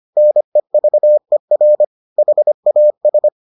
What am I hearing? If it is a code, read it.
Morse code, heard as a keyed tone, about 25 wpm, 600 Hz. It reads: NEVER HAS